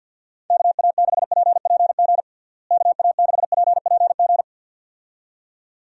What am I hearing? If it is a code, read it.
Morse code, heard as a keyed tone, about 48 wpm, 700 Hz. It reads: KA6PPG KA6PPG
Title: USA Call Signs - 6 Characters - Repeated 2x - 50wpm